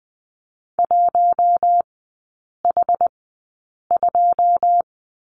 Morse code, heard as a keyed tone, about 20 wpm, spaced as for 8 wpm, 700 Hz. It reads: 1H2
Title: 081 Lesson 14e - Number 2 - Three Characters 20wpm